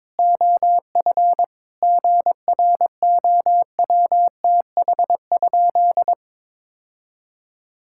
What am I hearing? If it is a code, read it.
Morse code, heard as a keyed tone, about 22 wpm, 700 Hz. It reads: OF GROWTH?
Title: Sentences from Top 1000 Words - Review Part 5 - 22wpm